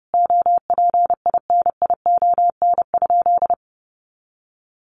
Morse code, heard as a keyed tone, about 30 wpm, 700 Hz. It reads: OPINION?
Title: Sentences from Top 1200 Words 30wpm